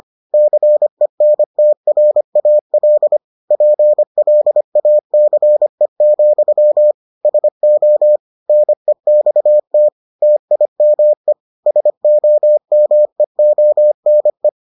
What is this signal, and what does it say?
Morse code, heard as a keyed tone, about 25 wpm, 600 Hz. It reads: CENTRAL PLACE, SO NEXT TIME SOMEONE